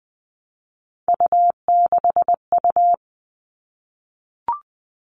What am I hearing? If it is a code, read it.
Morse code, heard as a keyed tone, about 20 wpm, 700 Hz. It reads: U6U
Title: All Cut Numbers - 3-Digit 20wpm